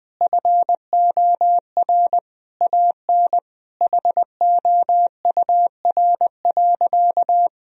Morse code, heard as a keyed tone, about 21 wpm, 700 Hz. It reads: FOR AN HOUR.
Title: Sentences from Top 1000 Words - Review Part 4 - 20wpm